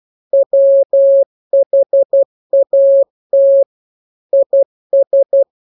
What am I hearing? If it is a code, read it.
Morse code, heard as a keyed tone, about 12 wpm, 550 Hz. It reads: WHAT IS